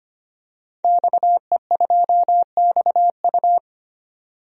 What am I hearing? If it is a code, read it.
Morse code, heard as a keyed tone, about 25 wpm, 700 Hz. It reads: XE2XU
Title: DXCC-100 Call Signs - 5 Characters - 25wpm